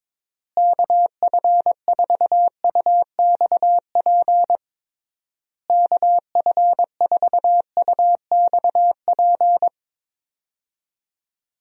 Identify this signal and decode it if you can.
Morse code, heard as a keyed tone, about 22 wpm, 700 Hz. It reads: KF4UXP KF4UXP
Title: USA Call Signs - 6 Characters - Repeated 3x - 22wpm